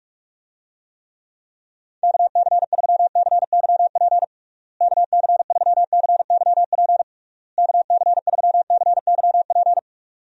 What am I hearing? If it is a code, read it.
Morse code, heard as a keyed tone, about 41 wpm, 700 Hz. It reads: KC3CYP KC3CYP KC3CYP
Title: USA Call Signs - 6 Characters - Repeated 3x - 45wpm